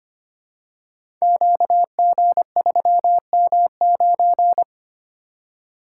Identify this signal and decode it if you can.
Morse code, heard as a keyed tone, about 25 wpm, 700 Hz. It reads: QG3M9
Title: Code Groups - 5 Letter:Numbers - 25wpm